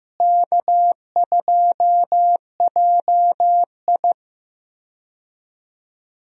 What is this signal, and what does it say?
Morse code, heard as a keyed tone, about 15 wpm, 700 Hz. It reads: K2JI
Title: USA Call Signs - 4 Characters - Repeated 3x - 15wpm